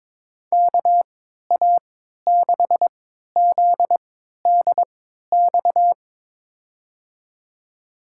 Morse code, heard as a keyed tone, about 23 wpm, spaced as for 13 wpm, 700 Hz. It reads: KA6ZDX
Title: USA Call Signs - 6 Characters - 3x Character Spacing - 22wpm